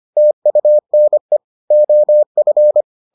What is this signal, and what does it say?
Morse code, heard as a keyed tone, about 25 wpm, 600 Hz. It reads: TUNE OF